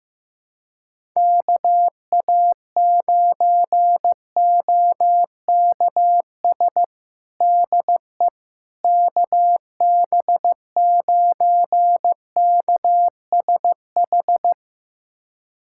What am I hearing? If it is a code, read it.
Morse code, heard as a keyed tone, about 15 wpm, 700 Hz. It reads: KA9OKS DE KB9KSH
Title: USA Callsign DE Callsign - 6 Characters - 15wpm